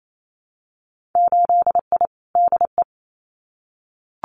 Morse code, heard as a keyed tone, about 28 wpm, 700 Hz. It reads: 8I DE
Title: Winter Field Day Exchanges Encore 28wpm